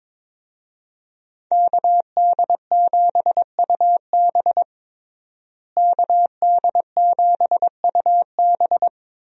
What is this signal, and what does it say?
Morse code, heard as a keyed tone, about 22 wpm, 700 Hz. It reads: KD7UB KD7UB
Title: USA Call Signs - 5 Characters - Repeated 2x - 22wpm